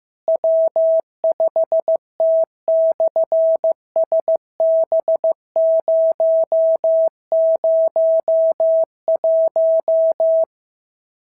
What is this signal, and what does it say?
Morse code, heard as a keyed tone, about 14 wpm, 650 Hz. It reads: W5T/SB001
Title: SOTA QSO Practice 15wpm